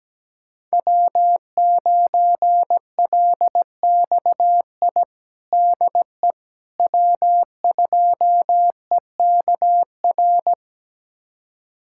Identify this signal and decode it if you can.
Morse code, heard as a keyed tone, about 17 wpm, 700 Hz. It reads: W9LXI DE W2EKR
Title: USA Callsign DE Callsign - 5 Characters - 17wpm